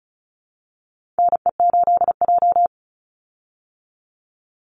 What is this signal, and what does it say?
Morse code, heard as a keyed tone, about 34 wpm, 700 Hz. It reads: NE8J
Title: Call Signs Repeated 3x - 35wpm